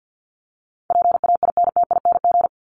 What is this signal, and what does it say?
Morse code, heard as a keyed tone, about 50 wpm, 700 Hz. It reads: PAINTING
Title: ING Suffix - Top 100 Words with Base Words 50wpm